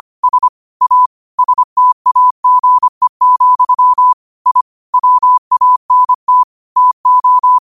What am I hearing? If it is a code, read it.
Morse code, heard as a keyed tone, about 25 wpm, 1000 Hz. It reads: S A STAGE, I WANT TO
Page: New quotes every day in morse code at 25 Words per minute.